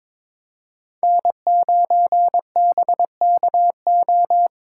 Morse code, heard as a keyed tone, about 26 wpm, 700 Hz. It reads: N9BKO
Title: Speed Racing - USA 1x3 Call Signs 22wpm